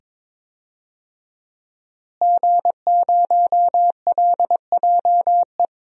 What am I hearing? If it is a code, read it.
Morse code, heard as a keyed tone, about 22 wpm, 700 Hz. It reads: G0LJE
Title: DXCC-100 Call Signs - 5 Characters - Repeated 2x - 22wpm